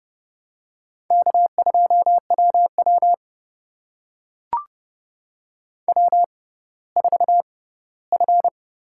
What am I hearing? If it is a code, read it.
Morse code, heard as a keyed tone, about 30 wpm, spaced as for 17 wpm, 690 Hz. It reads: K2WW W4F
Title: USA Call Signs - 4 Characters - 6x Character Spacing - 30wpm